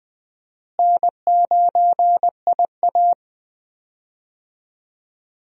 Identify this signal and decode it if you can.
Morse code, heard as a keyed tone, about 20 wpm, 700 Hz. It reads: N9IA
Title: USA Call Signs - 4 Characters - Repeated 2x - 20wpm